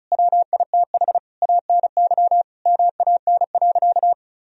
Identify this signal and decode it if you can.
Morse code, heard as a keyed tone, about 34 wpm, 700 Hz. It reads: WITH ANY MAN.
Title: Sentences from Top 1000 Words - Review Part 1 - 35wpm